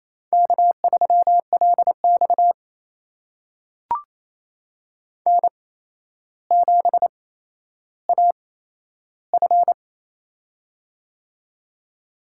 Morse code, heard as a keyed tone, about 28 wpm, spaced as for 12 wpm, 700 Hz. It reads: K3LX N7AF
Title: USA Call Signs - 4 Characters - 8x Character Spacing - 28wpm